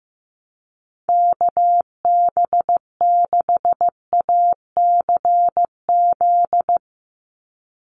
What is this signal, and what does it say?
Morse code, heard as a keyed tone, about 16 wpm, 700 Hz. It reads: KB6ACZ
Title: USA Call Signs 15wpm